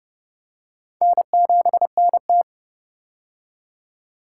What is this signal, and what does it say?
Morse code, heard as a keyed tone, about 30 wpm, 700 Hz. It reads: N7NT
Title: K1USN SST QSO Practice - 30wpm